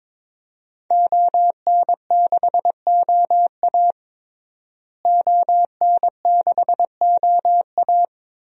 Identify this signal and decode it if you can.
Morse code, heard as a keyed tone, about 22 wpm, 700 Hz. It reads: ON6OA ON6OA
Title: DXCC-100 Call Signs - 5 Characters - Repeated 3x - 22wpm